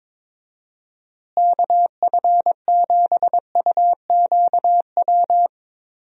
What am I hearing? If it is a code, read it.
Morse code, heard as a keyed tone, about 22 wpm, 700 Hz. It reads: KF7UQW
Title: USA Call Signs - 6 Characters - Repeated 2x - 22wpm